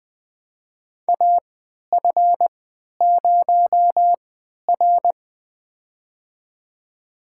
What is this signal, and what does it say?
Morse code, heard as a keyed tone, about 20 wpm, spaced as for 11 wpm, 700 Hz. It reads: AF0R
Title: USA Call Signs - 3x Character Spacing - 20wpm